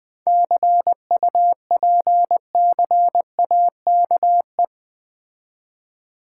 Morse code, heard as a keyed tone, about 20 wpm, 690 Hz. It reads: CUPCAKE
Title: Difficult Words with Hints - 20wpm